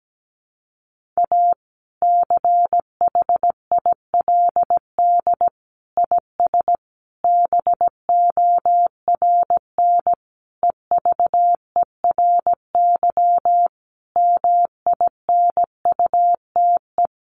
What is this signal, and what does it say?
Morse code, heard as a keyed tone, about 17 wpm, 700 Hz. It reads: A CHILD IS BORN EVERY MINUTE
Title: Sentences from Top 1400 Words 17wpm